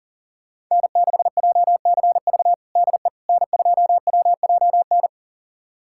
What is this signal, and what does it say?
Morse code, heard as a keyed tone, about 40 wpm, 700 Hz. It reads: N6JCV DE N2WJN
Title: USA Callsign DE Callsign - 5 Characters - 40wpm